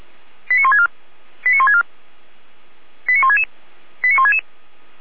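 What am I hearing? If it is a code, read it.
Five-tone selective call (ZVEI), heard as quick runs of five tones, tones of about 70 ms. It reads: 89164 89164 89166 89166
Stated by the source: ZVEI five-tone selective call received with an SDR, logged by feuerwehr-orlamuende.org fire brigade.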